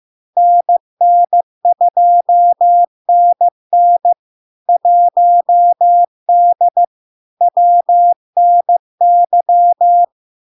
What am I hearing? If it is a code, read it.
Morse code, heard as a keyed tone, about 15 wpm, 700 Hz. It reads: NN2NN 1D WNY
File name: LARK-FD-2022-Log-in-CW-15wpm-700-Hz